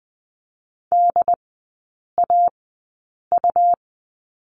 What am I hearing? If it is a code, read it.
Morse code, heard as a keyed tone, about 20 wpm, spaced as for 8 wpm, 700 Hz. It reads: DAU
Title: 127 Lesson 21e - Number 6 - Three Characters 20wpm